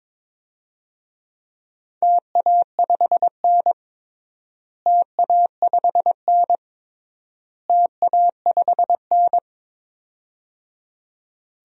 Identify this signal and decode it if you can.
Morse code, heard as a keyed tone, about 22 wpm, 700 Hz. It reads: TA5N TA5N TA5N
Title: DXCC-100 Call Signs - 4 Characters - Repeated 3x - 22wpm